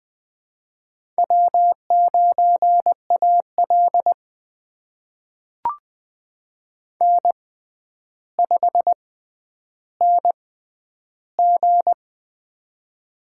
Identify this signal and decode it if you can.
Morse code, heard as a keyed tone, about 21 wpm, spaced as for 10 wpm, 700 Hz. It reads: W9AL N5NG
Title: USA Call Signs - 4 Characters - 6x Character Spacing - 20wpm